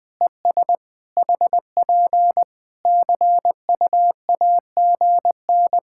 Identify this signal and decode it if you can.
Morse code, heard as a keyed tone, about 20 wpm, 700 Hz. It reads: ES HP CUAGN
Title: Practice QSO - Exchange 4 - Ending - 20wpm